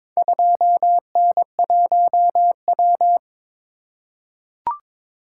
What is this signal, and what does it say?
Morse code, heard as a keyed tone, about 22 wpm, 700 Hz. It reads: 2N1W
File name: practice / 22wpm / Four Letter-Number Combinations - 22wpm